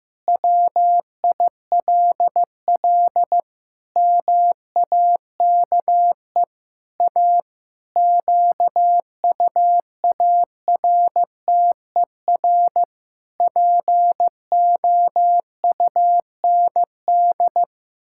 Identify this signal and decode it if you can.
Morse code, heard as a keyed tone, about 15 wpm, 700 Hz. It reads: WILL MAKE A QUARTER POUND
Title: Pangram Challenge v2 - 15wpm